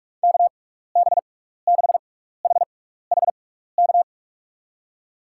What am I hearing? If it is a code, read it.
Morse code, heard as a keyed tone, about 44 wpm, spaced as for 15 wpm, 700 Hz. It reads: KB6HHK
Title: USA Call Signs - 6 Characters - 6x Character Spacing - 45wpm